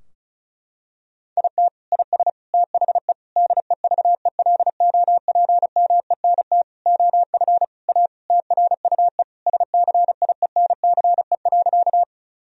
Morse code, heard as a keyed tone, about 34 wpm, 700 Hz. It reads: IT IS THE DEVELOPMENT OF A TRUE SCIENCE.
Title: Sentences from Top 1000 Words - Review Part 4 - 35wpm